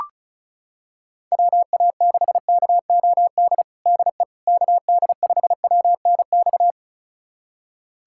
Morse code, heard as a keyed tone, about 35 wpm, 700 Hz. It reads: WA6KOD DE KD5WNX
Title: USA Callsign DE Callsign - 6 Characters - 35wpm